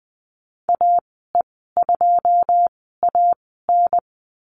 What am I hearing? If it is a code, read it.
Morse code, heard as a keyed tone, about 20 wpm, spaced as for 15 wpm, 700 Hz. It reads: AE2AN
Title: 154 Lesson 25g - Letter p - Call Signs 20wpm